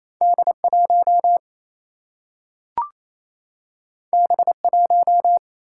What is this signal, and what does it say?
Morse code, heard as a keyed tone, about 28 wpm, 700 Hz. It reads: D 1 B 1
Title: Single-Letter Single-Number 28wpm